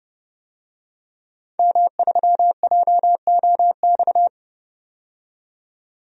Morse code, heard as a keyed tone, about 31 wpm, 700 Hz. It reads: M3JOX
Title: DXCC-100 Call Signs - 5 Characters - 30wpm